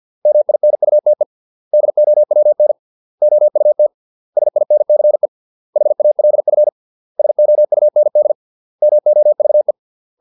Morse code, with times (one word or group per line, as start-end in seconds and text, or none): 0.3-1.2 MINUTE
1.7-2.7 DOWN
3.2-3.9 OUT
4.4-5.3 SINCE
5.8-6.7 HALF
7.2-8.3 SOUND
8.8-9.7 MOVE